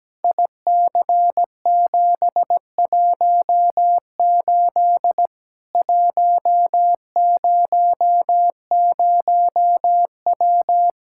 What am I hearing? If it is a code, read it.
Morse code, heard as a keyed tone, about 17 wpm, 700 Hz. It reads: IC718 100W
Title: Practice QSO - Exchange 2 - Info - 17wpm